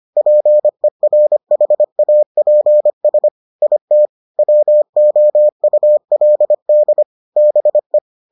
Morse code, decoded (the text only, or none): PERHAPS IT WOULD BE